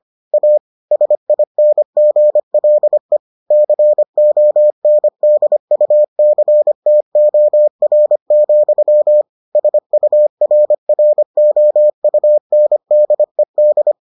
Morse code, heard as a keyed tone, about 25 wpm, 600 Hz. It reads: A SINGLE CONDUCTOR, SURROUNDED